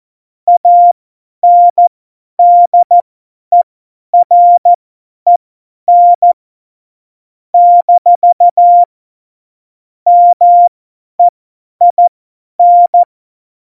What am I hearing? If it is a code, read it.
Morse code, heard as a keyed tone, about 14 wpm, spaced as for 10 wpm, 700 Hz. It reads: ANDEREN - MEIN
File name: Morsekurs / Witze / Witz 25. Dezember, 50 F70